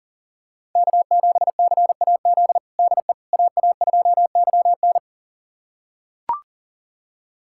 Morse code, heard as a keyed tone, about 40 wpm, 700 Hz. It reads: K7CAZ DE AA2YN
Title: USA Callsign DE Callsign - 5 Characters - 40wpm